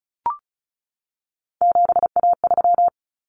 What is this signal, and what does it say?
Morse code, heard as a keyed tone, about 34 wpm, 700 Hz. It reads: 7A3
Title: Common Cut Numbers - 3-Digit 35wpm